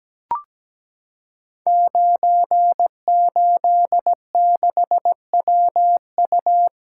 Morse code, heard as a keyed tone, about 17 wpm, 700 Hz. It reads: 986WU
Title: Code Groups - 5 Letter:Numbers - 17wpm